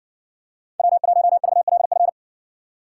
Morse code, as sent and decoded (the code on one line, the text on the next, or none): .-- .---- ...- .-.. ..-.
W1VLF